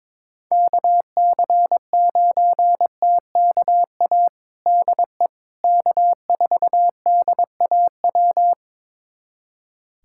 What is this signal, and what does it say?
Morse code, heard as a keyed tone, about 21 wpm, 700 Hz. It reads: KC9TKA DE K4DAW
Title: USA Callsign DE Callsign 22wpm